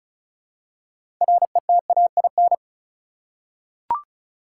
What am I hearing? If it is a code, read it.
Morse code, heard as a keyed tone, about 36 wpm, 700 Hz. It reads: RETAIN
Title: Six Letter Words - 35wpm